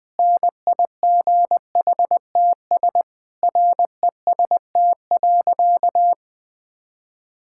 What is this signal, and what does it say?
Morse code, heard as a keyed tone, about 20 wpm, 700 Hz. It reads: NIGHTS REST.
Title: Difficult Sentences with Hints - 20wpm